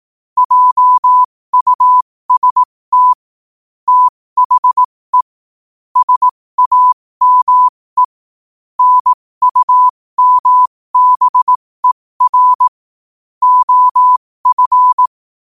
Morse code, heard as a keyed tone, about 18 wpm, spaced as for 15 wpm, 1000 Hz. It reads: JUST THE SAME NUMBER OF
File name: qotdfiles / 15wpm